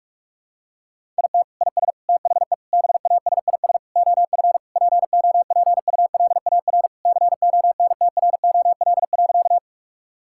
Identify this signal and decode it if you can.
Morse code, heard as a keyed tone, about 41 wpm, 700 Hz. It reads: IT IS THE BASIS OF POPULAR CONTROL.
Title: Sentences from Top 1000 Words - Review Part 5 - 45wpm